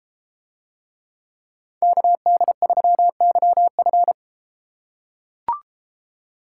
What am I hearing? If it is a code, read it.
Morse code, heard as a keyed tone, about 33 wpm, 700 Hz. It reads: KD3YF